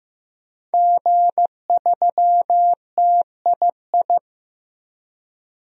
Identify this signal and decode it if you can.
Morse code, heard as a keyed tone, about 15 wpm, 700 Hz. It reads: G3TII
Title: DXCC-100 Call Signs - 5 Characters - 15wpm